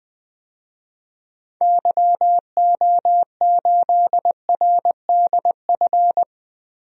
Morse code, heard as a keyed tone, about 20 wpm, 700 Hz. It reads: YO8RDF
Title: DXCC-100 Call Signs - 6 Characters - Repeated 2x - 20wpm